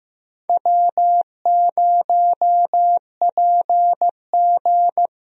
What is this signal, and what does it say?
Morse code, heard as a keyed tone, about 16 wpm, 700 Hz. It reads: W0PG
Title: USA Call Signs - 4 Characters - 15wpm